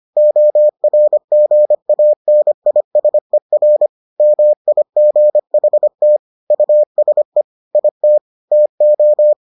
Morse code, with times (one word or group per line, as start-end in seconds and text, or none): 0.2-3.9 ORGANISER
4.2-6.2 MIGHT
6.5-7.4 USE
7.8-8.2 IT
8.5-9.3 TO